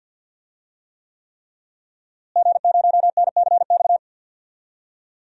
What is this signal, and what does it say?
Morse code, heard as a keyed tone, about 50 wpm, 700 Hz. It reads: G0NCX
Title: DXCC-100 Call Signs - 5 Characters - Repeated 3x - 50wpm